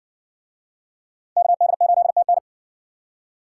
Morse code, heard as a keyed tone, about 62 wpm, 700 Hz. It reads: KD8TN